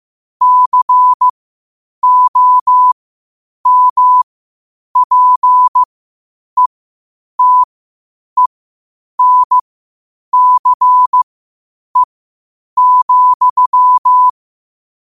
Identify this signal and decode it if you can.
Morse code, heard as a keyed tone, about 15 wpm, spaced as for 8 wpm, 1000 Hz. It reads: COMPETENCE,
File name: qotdfiles / 8wpm